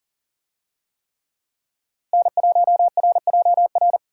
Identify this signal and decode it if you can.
Morse code, heard as a keyed tone, about 40 wpm, 700 Hz. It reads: N1RJR
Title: USA Call Signs - 5 Characters - Repeated 3x - 40wpm